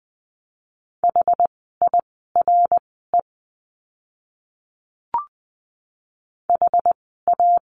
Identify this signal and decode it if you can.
Morse code, heard as a keyed tone, about 20 wpm, spaced as for 14 wpm, 700 Hz. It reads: HIRE HA
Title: 061 Lesson 11f - Letter h - Words 20wpm